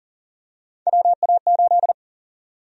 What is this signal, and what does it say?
Morse code, heard as a keyed tone, about 40 wpm, 700 Hz. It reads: WA8
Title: USA Call Sign Prefixes - 40wpm